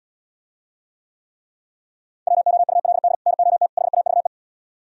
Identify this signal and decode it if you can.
Morse code, heard as a keyed tone, about 77 wpm, 700 Hz. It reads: WOULD NOT HAVE